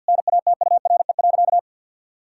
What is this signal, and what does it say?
Morse code, heard as a keyed tone, about 50 wpm, 700 Hz. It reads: NATURE.
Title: Sentences from Top 1000 Words - Review Part 6 - 50wpm